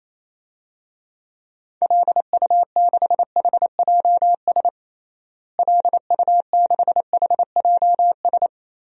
Morse code, heard as a keyed tone, about 28 wpm, 700 Hz. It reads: LU6HJS LU6HJS
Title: DXCC-100 Call Signs - 6 Characters - Repeated 3x - 28wpm